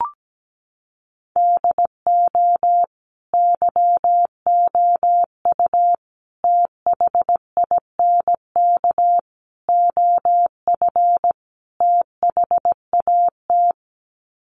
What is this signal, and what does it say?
Morse code, heard as a keyed tone, about 17 wpm, 700 Hz. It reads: DO YOU THINK OF THAT
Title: Sets of 5 Words Encore2 17wpm